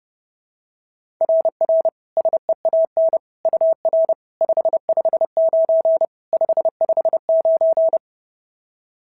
Morse code, heard as a keyed tone, about 29 wpm, 690 Hz. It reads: RR SEAN UR 559 559
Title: SOTA QSO Practice 30wpm